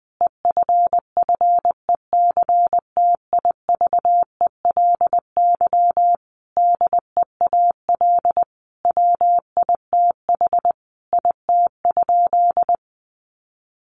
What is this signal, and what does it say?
Morse code, heard as a keyed tone, about 20 wpm, 700 Hz. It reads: EFFECTIVELY DEAL WITH IT?
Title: Sentences from Top 1400 Words 20wpm